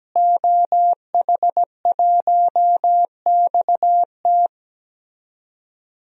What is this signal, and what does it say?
Morse code, heard as a keyed tone, about 17 wpm, 700 Hz. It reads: OH1XT
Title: DXCC-100 Call Signs - 5 Characters - Repeated 3x - 17wpm